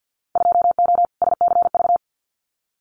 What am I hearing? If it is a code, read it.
Morse code, heard as a keyed tone, about 50 wpm, 700 Hz. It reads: 2O SCV
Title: Winter Field Day Exchanges Encore 50wpm